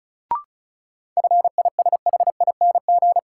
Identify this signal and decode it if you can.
Morse code, heard as a keyed tone, about 35 wpm, 700 Hz. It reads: FISHING
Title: Sentences from Top 1000 Words - Review Part 5 - New Word Drill - 35wpm